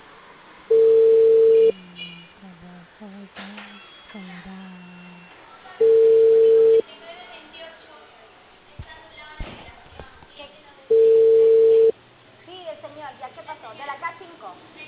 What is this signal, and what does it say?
Ringback tone (China, 450 Hz), with speech.